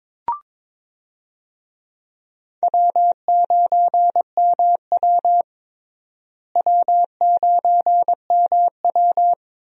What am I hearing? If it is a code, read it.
Morse code, heard as a keyed tone, about 22 wpm, 700 Hz. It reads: W9MW W9MW
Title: USA Call Signs - 4 Characters - Repeated 2x - 22wpm